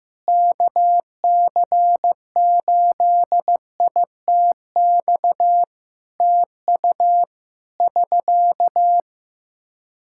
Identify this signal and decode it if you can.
Morse code, heard as a keyed tone, about 15 wpm, 700 Hz. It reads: KC8ITX TU <SK>
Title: K1USN SST QSO Practice 15wpm